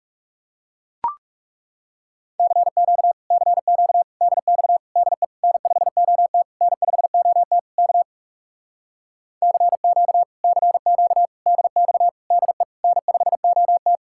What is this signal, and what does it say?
Morse code, heard as a keyed tone, about 40 wpm, 700 Hz. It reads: CQ CQ DX DE N5OT N5OT K CQ CQ DX DE N5OT